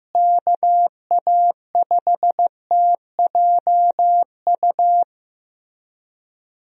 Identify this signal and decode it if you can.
Morse code, heard as a keyed tone, about 15 wpm, 700 Hz. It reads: KA5TJU
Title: USA Call Signs - 6 Characters - Repeated 2x - 15wpm